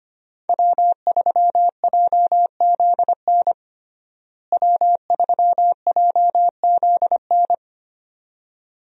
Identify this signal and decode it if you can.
Morse code, heard as a keyed tone, about 25 wpm, 700 Hz. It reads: W3JZN W3JZN
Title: USA Call Signs - 5 Characters - Repeated 2x - 25wpm